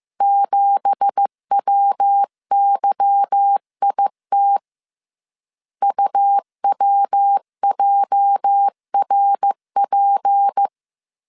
Morse code, heard as a keyed tone, about 15 wpm, 800 Hz. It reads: 7WYIT UWJRP
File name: CW / DET / 1 9 1500C S C 14wpm 3245